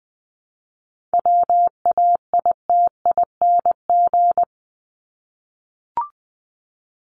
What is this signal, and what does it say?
Morse code, heard as a keyed tone, about 20 wpm, 700 Hz. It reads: WAITING
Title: ING Suffix - Top 100 Words with Base Words 20wpm